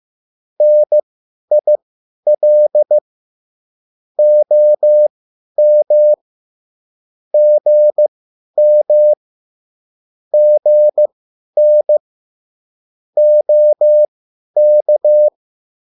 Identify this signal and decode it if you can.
Morse code, heard as a keyed tone, about 15 wpm, spaced as for 10 wpm, 600 Hz. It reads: NIL OM GM GN OK
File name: cwbook